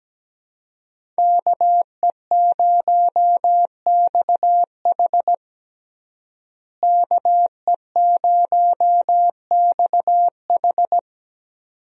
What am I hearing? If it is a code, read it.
Morse code, heard as a keyed tone, about 17 wpm, 700 Hz. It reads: KE0XH KE0XH
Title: USA Call Signs - 5 Characters - Repeated 2x - 17wpm